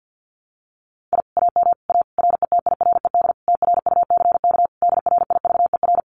Morse code, heard as a keyed tone, about 50 wpm, 700 Hz. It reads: I AM A RETIRED TRUCK DRIVER
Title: Common QSO Phrases Encore 50wpm